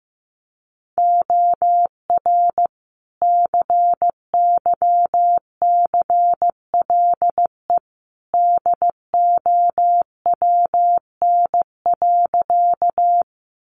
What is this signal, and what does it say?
Morse code, heard as a keyed tone, about 15 wpm, 700 Hz. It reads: OR CYCLE DOWN.
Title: Sentences from Top 900 Words 15wpm